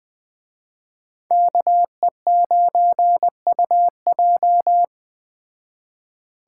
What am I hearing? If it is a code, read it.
Morse code, heard as a keyed tone, about 20 wpm, 700 Hz. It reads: KE9UJ